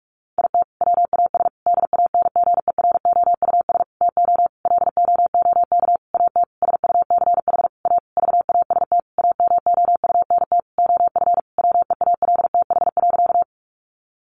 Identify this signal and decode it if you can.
Morse code, heard as a keyed tone, about 44 wpm, 700 Hz. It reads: IT WAS DANGEROUS TO LOOK AT SUCH A VAST AMOUNT OF WEALTH.
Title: Sentences from Top 1300 Words 45wpm